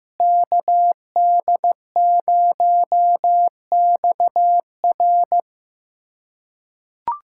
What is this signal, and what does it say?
Morse code, heard as a keyed tone, about 15 wpm, 700 Hz. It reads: KD0XR
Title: USA Callsign DE Callsign - 5 Characters - 15wpm